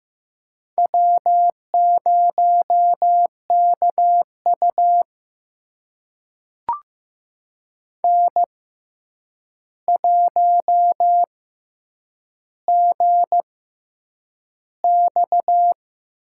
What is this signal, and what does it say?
Morse code, heard as a keyed tone, about 15 wpm, spaced as for 8 wpm, 700 Hz. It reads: W0KU N1GX
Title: USA Call Signs - 6x Character Spacing - 15wpm